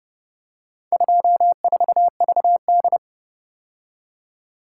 Morse code, heard as a keyed tone, about 30 wpm, 700 Hz. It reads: 24VD